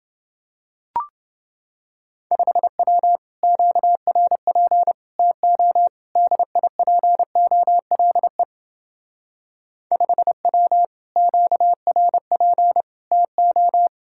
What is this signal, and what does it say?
Morse code, heard as a keyed tone, about 28 wpm, 700 Hz. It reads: 5W QRP TO DIPOLE 5W QRP TO